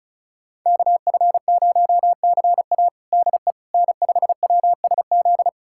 Morse code, heard as a keyed tone, about 36 wpm, 700 Hz. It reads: KF0CA DE N5WSZ
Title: USA Callsign DE Callsign - 5 Characters - 35wpm